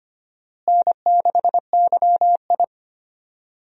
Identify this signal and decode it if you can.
Morse code, heard as a keyed tone, about 25 wpm, 700 Hz. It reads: N6YI
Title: USA Call Signs - 4 Characters - Repeated 2x - 25wpm